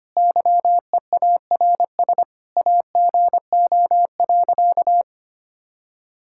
Morse code, heard as a keyed tone, about 25 wpm, 700 Hz. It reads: YEARS AGO.
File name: practice / 25wpm / Sentences with 4-Word Phrases - 25wpm